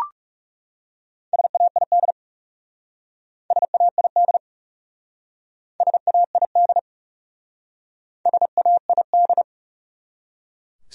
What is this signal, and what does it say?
Morse code, heard as a keyed tone, about 38 wpm, 700 Hz. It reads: SAID SAID SAID SAID